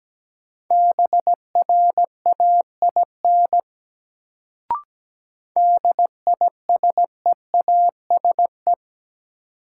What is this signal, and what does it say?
Morse code, heard as a keyed tone, about 17 wpm, 700 Hz. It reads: BRAIN DISEASE
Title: Sentences from Top 1000 Words - Review Part 6 - New Word Drill - 17wpm